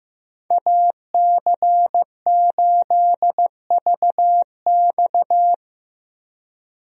Morse code, heard as a keyed tone, about 15 wpm, 700 Hz. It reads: AC8VX
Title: USA Call Signs - 5 Characters - 15wpm